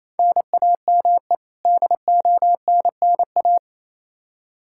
Morse code, heard as a keyed tone, about 28 wpm, 700 Hz. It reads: NAME DONNA
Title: Practice QSO - Exchange 1 - Beginning - 28wpm